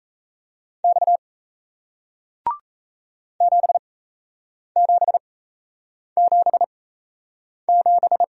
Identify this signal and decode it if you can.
Morse code, heard as a keyed tone, about 38 wpm, spaced as for 8 wpm, 700 Hz. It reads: X 7777